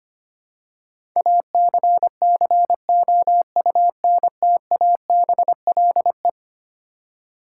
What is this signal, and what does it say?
Morse code, heard as a keyed tone, about 25 wpm, 700 Hz. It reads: ACCOUNTABLE
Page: morse-code-ninja-practice-25wpm